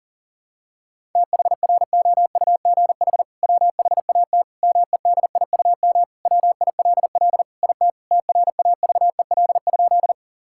Morse code, heard as a keyed tone, about 40 wpm, 700 Hz. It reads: THROUGH WHAT MEDIUM WILL IT TRAVEL?
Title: Sentences from Top 1000 Words - Review Part 6 - 40wpm